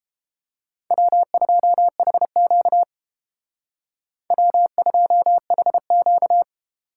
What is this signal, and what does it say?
Morse code, heard as a keyed tone, about 29 wpm, 700 Hz. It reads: W2HQ W2HQ